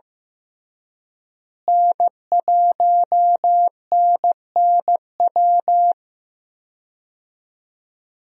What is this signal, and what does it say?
Morse code, heard as a keyed tone, about 15 wpm, 700 Hz. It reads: N1NNW
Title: USA Call Signs - 5 Characters - Repeated 3x - 15wpm